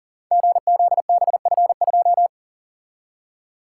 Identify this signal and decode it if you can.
Morse code, heard as a keyed tone, about 44 wpm, 690 Hz. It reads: GZBF2
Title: Code Groups - 5 Letter:Numbers - 40wpm